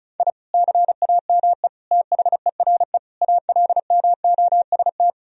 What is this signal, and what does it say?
Morse code, heard as a keyed tone, about 36 wpm, 700 Hz. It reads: I CAME THERE ALMOST
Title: Sentences from Top 1000 Words - Review Part 4 - 35wpm